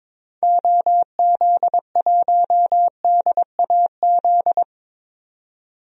Morse code, heard as a keyed tone, about 22 wpm, 700 Hz. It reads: OZ1DAZ
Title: DXCC-100 Call Signs - 6 Characters - 22wpm